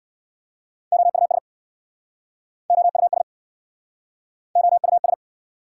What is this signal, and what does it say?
Morse code, heard as a keyed tone, about 69 wpm, 700 Hz. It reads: OUS OUS OUS